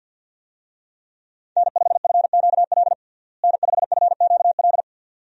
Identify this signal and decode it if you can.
Morse code, heard as a keyed tone, about 50 wpm, 700 Hz. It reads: N5FQL N5FQL
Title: USA Call Signs - 5 Characters - Repeated 2x - 50wpm